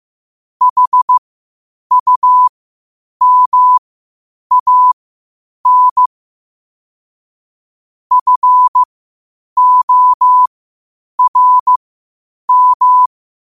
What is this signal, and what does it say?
Morse code, heard as a keyed tone, about 15 wpm, spaced as for 8 wpm, 1000 Hz. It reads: HUMAN FORM
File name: qotdfiles / 8wpm